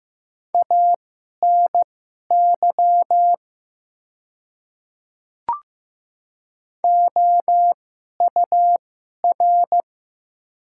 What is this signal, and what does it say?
Morse code, heard as a keyed tone, about 15 wpm, spaced as for 11 wpm, 700 Hz. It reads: ANY OUR
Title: 2x Character Spacing - Top 200 Words - 15wpm